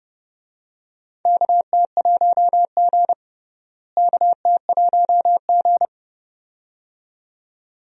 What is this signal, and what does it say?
Morse code, heard as a keyed tone, about 30 wpm, 700 Hz. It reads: KT1G KT1G
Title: USA Call Signs - 4 Characters - Repeated 2x - 30wpm